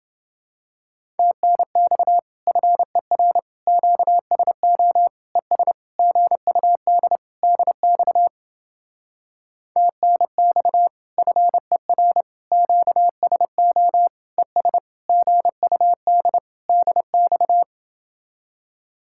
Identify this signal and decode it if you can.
Morse code, heard as a keyed tone, about 29 wpm, 700 Hz. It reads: TNX FER QSO ES GUD DX TNX FER QSO ES GUD DX